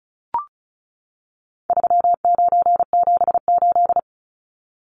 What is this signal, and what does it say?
Morse code, heard as a keyed tone, about 34 wpm, 700 Hz. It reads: 3978
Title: Numbers - 4-Digit 35wpm